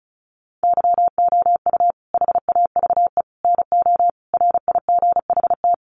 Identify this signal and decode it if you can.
Morse code, heard as a keyed tone, about 34 wpm, 700 Hz. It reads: YOU HAVE NO RIGHT
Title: Sentences from Top 1100 Words 35wpm